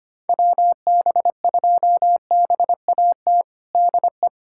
Morse code, heard as a keyed tone, about 25 wpm, 700 Hz. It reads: WB2BAT DE
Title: USA Callsign DE Callsign - 6 Characters - 25wpm